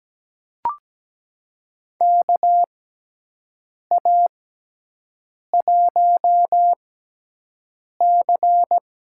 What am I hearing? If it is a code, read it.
Morse code, heard as a keyed tone, about 17 wpm, spaced as for 6 wpm, 700 Hz. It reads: KA1C
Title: USA Call Signs - 6 Characters - 6x Character Spacing - 17wpm